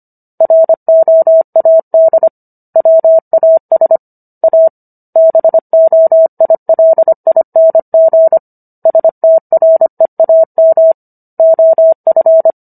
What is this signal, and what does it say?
Morse code, heard as a keyed tone, about 25 wpm, 650 Hz. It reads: ROAD WAS A BOILING STREAM OF